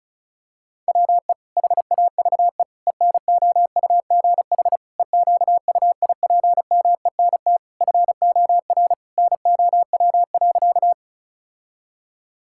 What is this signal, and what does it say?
Morse code, heard as a keyed tone, about 36 wpm, 700 Hz. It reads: WE HAVE ENOUGH EQUIPMENT FOR NOW.
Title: Sentences from Top 1000 Words - Review Part 5 - 35wpm